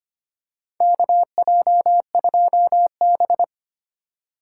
Morse code, heard as a keyed tone, about 25 wpm, 700 Hz. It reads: KJ2B
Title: USA Call Signs - 4 Characters - 4x Character Spacing - 25wpm